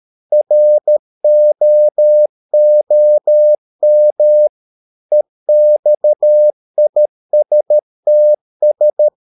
Morse code, heard as a keyed tone, about 13 wpm, 600 Hz. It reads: ROOM EXISTS
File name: Never 0013wpm